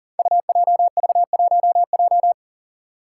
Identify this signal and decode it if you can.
Morse code, heard as a keyed tone, about 39 wpm, 700 Hz. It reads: UJV1J
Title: Five Challenging Letter-Number Combinations 40wpm